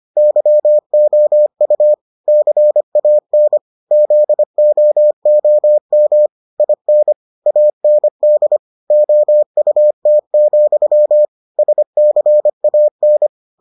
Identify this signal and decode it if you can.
Morse code, heard as a keyed tone, about 25 wpm, 600 Hz. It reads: YOU CAN ZOOM IN AND OUT, SCAN